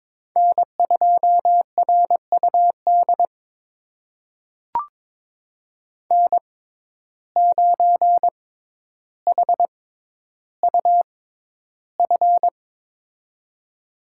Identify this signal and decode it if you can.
Morse code, heard as a keyed tone, about 23 wpm, spaced as for 11 wpm, 700 Hz. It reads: N2RUD N9HUF
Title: USA Call Signs - 5 Characters - 6x Character Spacing - 22wpm